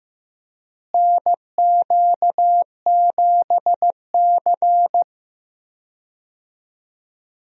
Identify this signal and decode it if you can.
Morse code, heard as a keyed tone, about 15 wpm, 700 Hz. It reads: NQ7C
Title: USA Call Signs - 4 Characters - Repeated 3x - 15wpm